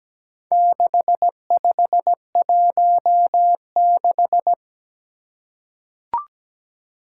Morse code, heard as a keyed tone, about 17 wpm, 700 Hz. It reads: 6516
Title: Four Challenging Letter-Number Combinations 17wpm